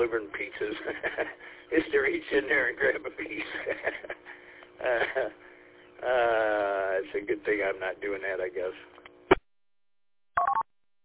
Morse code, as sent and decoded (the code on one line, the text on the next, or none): none